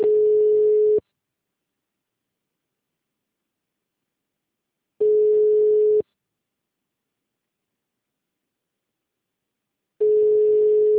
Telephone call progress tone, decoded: ringback tone

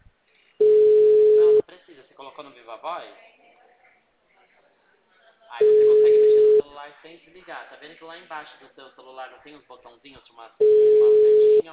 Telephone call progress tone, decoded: ringback tone

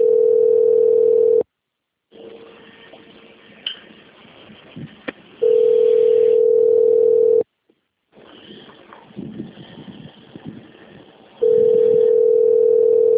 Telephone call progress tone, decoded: ringback tone

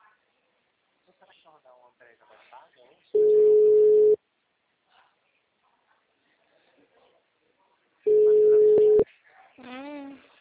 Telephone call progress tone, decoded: ringback tone

